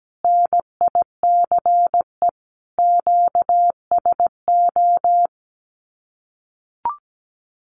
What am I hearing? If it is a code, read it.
Morse code, heard as a keyed tone, about 17 wpm, 700 Hz. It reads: NICE QSO
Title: Common QSO Phrases Encore 17wpm